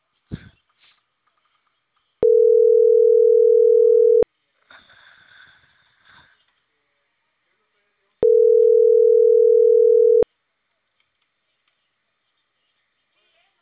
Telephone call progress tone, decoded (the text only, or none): ringback tone